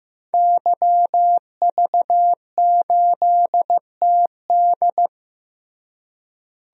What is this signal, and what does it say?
Morse code, heard as a keyed tone, about 15 wpm, 700 Hz. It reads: YV8TD
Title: DXCC-100 Call Signs - 5 Characters - Repeated 2x - 15wpm